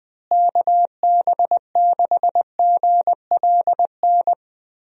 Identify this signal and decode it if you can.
Morse code, heard as a keyed tone, about 20 wpm, 700 Hz. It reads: KB6GLN
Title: USA Call Signs - 6 Characters - Repeated 3x - 20wpm